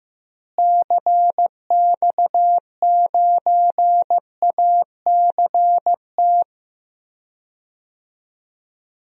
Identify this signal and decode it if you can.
Morse code, heard as a keyed tone, about 15 wpm, 700 Hz. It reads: CX9ACT
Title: DXCC-100 Call Signs - 6 Characters - Repeated 2x - 15wpm